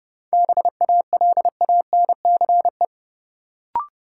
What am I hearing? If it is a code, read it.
Morse code, heard as a keyed tone, about 30 wpm, 700 Hz. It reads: BALANCE
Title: Sentences from Top 900 Words - New Word Drill - 30wpm